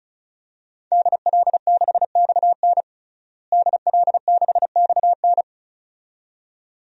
Morse code, heard as a keyed tone, about 36 wpm, 700 Hz. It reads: DL6XN DL6XN
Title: DXCC-100 Call Signs - 5 Characters - Repeated 2x - 35wpm